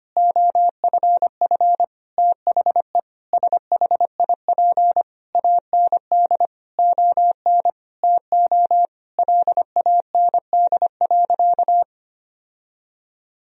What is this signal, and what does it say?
Morse code, heard as a keyed tone, about 25 wpm, 700 Hz. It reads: OFF THE SHIP AND ON TO LAND.
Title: Sentences from Top 1000 Words - Review Part 2 - 25wpm